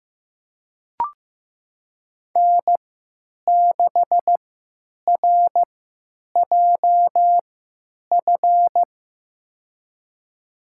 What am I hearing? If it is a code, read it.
Morse code, heard as a keyed tone, about 15 wpm, spaced as for 8 wpm, 700 Hz. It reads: N6RJF
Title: USA Call Signs - 5 Characters - 3x Character Spacing - 15wpm